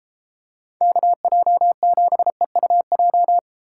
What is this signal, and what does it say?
Morse code, heard as a keyed tone, about 33 wpm, 700 Hz. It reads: KJ7EUJ